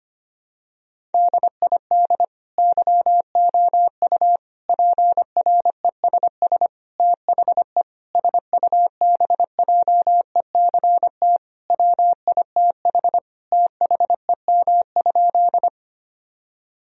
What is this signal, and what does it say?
Morse code, heard as a keyed tone, about 25 wpm, 700 Hz. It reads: DID YOU PRESS THE SUBJECT WITH THEM?
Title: Sentences from Top 1000 Words - Review Part 1 - 25wpm